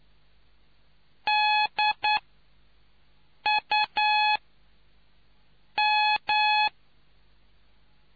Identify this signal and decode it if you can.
Morse code, heard as a keyed tone, about 9 wpm, 850 Hz. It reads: D U M